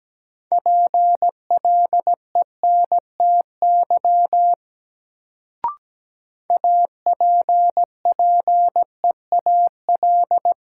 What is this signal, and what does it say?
Morse code, heard as a keyed tone, about 17 wpm, 700 Hz. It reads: PLENTY APPEAL
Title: Sentences from Top 1300 Words - New Word Drill Alt - 17wpm